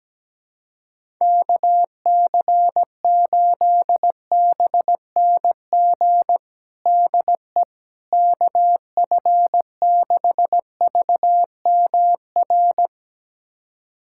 Morse code, heard as a keyed tone, about 17 wpm, 700 Hz. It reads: KC8BNG DE KF6VMR
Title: USA Callsign DE Callsign - 6 Characters - 17wpm